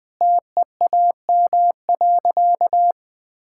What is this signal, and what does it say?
Morse code, heard as a keyed tone, about 20 wpm, 700 Hz. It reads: TEAM.
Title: Sentences with 5-Word Phrases - 20wpm